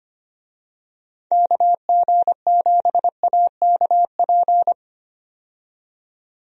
Morse code, heard as a keyed tone, about 25 wpm, 700 Hz. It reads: KG7AKP
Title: USA Call Signs - 6 Characters - 25wpm